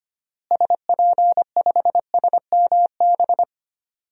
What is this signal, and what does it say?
Morse code, heard as a keyed tone, about 25 wpm, 700 Hz. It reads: SP5SMB
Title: DXCC-100 Call Signs - 6 Characters - Repeated 2x - 25wpm